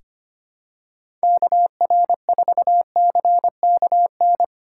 Morse code, heard as a keyed tone, about 25 wpm, 700 Hz. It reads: KR4CKN
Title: USA Call Signs - 6 Characters - 4x Character Spacing - 25wpm